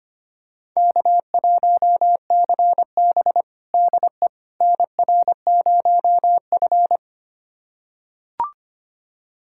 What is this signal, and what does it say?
Morse code, heard as a keyed tone, about 25 wpm, 700 Hz. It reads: K1CB DE NR0F
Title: USA Callsign DE Callsign - 4 Characters - 25wpm